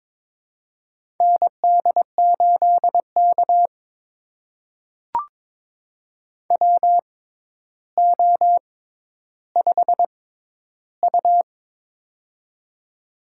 Morse code, heard as a keyed tone, about 23 wpm, spaced as for 11 wpm, 700 Hz. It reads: ND8K WO5U
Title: USA Call Signs - 4 Characters - 6x Character Spacing - 22wpm